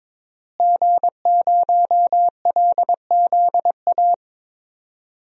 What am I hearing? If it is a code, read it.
Morse code, heard as a keyed tone, about 22 wpm, 700 Hz. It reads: G0LZA
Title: DXCC-100 Call Signs - 5 Characters - Repeated 2x - 22wpm